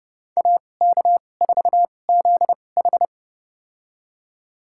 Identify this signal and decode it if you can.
Morse code, heard as a keyed tone, about 31 wpm, spaced as for 22 wpm, 700 Hz. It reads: AK4ZH
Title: USA Call Signs - 5 Characters - 2x Character Spacing - 30wpm